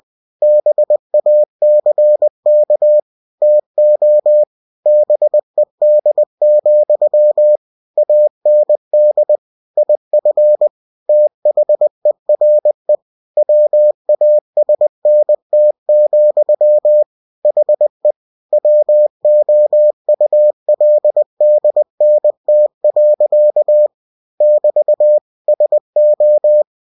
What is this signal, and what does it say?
Morse code, heard as a keyed tone, about 20 wpm, 600 Hz. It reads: BACK TO BED, AND IF THERE WASNT, HE WOULDNT. <BT> SO
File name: cw-books / winnie-the-pooh / winnie-20wpm-page-0041